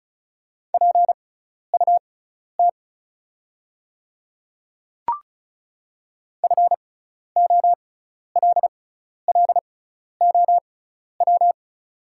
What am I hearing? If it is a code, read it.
Morse code, heard as a keyed tone, about 36 wpm, spaced as for 12 wpm, 690 Hz. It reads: PUT FOLLOW
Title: 6x Character Spacing - Top 300 Words - 35wpm